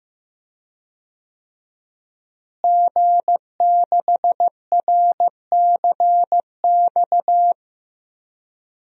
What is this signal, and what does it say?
Morse code, heard as a keyed tone, about 15 wpm, 700 Hz. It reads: G6RCX
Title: DXCC-100 Call Signs - 5 Characters - Repeated 3x - 15wpm